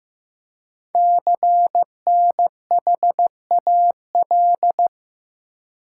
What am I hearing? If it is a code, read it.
Morse code, heard as a keyed tone, about 15 wpm, 700 Hz. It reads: CNHAL
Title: Code Groups - 5 Letters - 15wpm